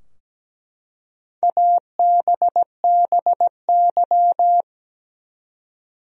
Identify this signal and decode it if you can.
Morse code, heard as a keyed tone, about 17 wpm, 700 Hz. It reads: ABBY
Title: USA Common Names - Max 5 Letters - 17wpm